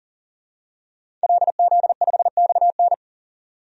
Morse code, heard as a keyed tone, about 40 wpm, 700 Hz. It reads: LZ5XN
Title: DXCC-100 Call Signs - 5 Characters - 40wpm